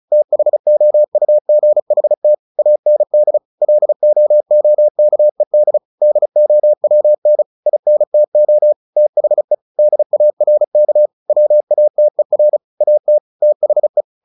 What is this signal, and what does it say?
Morse code, heard as a keyed tone, about 34 wpm, 600 Hz. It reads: THOUGHT AND LOOKED DOWN INTO THE DARK WATER AT THE
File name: the old man and the sea / the old man and the sea 1 35wpm